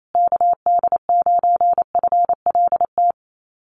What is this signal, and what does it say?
Morse code, heard as a keyed tone, about 28 wpm, 700 Hz. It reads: KD9FLT
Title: USA Call Signs Repeated 3x 28wpm